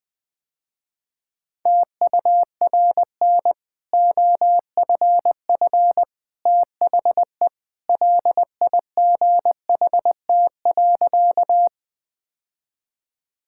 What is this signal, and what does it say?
Morse code, heard as a keyed tone, about 19 wpm, 700 Hz. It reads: TURN OFF THE LIGHT.
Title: Easy Sentences with Hints - 20wpm